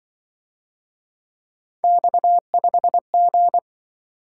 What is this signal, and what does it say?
Morse code, heard as a keyed tone, about 24 wpm, 700 Hz. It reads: X5G
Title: Speed-Racing - Three Letter-Number Combinations - 20wpm